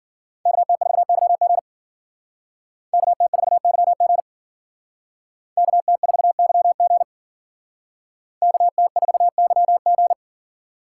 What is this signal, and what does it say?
Morse code, heard as a keyed tone, about 49 wpm, 700 Hz. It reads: KT4YG KT4YG KT4YG KT4YG
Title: Speed Racing - USA 2x2 Call Signs 40wpm